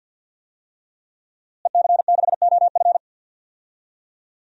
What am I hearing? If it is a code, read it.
Morse code, heard as a keyed tone, about 52 wpm, 700 Hz. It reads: EC6OF
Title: DXCC-100 Call Signs - 5 Characters - Repeated 2x - 50wpm